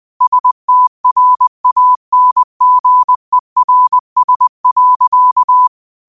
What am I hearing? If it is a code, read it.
Morse code, heard as a keyed tone, about 20 wpm, 1000 Hz. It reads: STRANGERS.